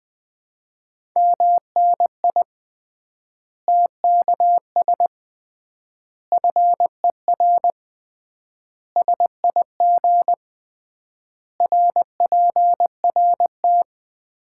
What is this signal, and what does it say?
Morse code, heard as a keyed tone, about 21 wpm, 700 Hz. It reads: MNI TKS FER SIG RPRT
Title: Common QSO Phrases - 3x Word-Spacing - 20wpm